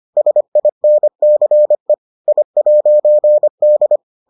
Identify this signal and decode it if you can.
Morse code, heard as a keyed tone, about 25 wpm, 600 Hz. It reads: SINCE I'D